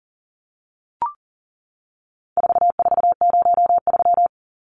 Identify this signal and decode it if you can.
Morse code, heard as a keyed tone, about 37 wpm, 700 Hz. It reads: 4403
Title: Numbers - 4-Digit 40wpm